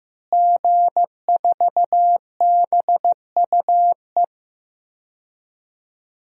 Morse code, heard as a keyed tone, about 15 wpm, 700 Hz. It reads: G4BUE
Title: DXCC-100 Call Signs - 5 Characters - Repeated 3x - 15wpm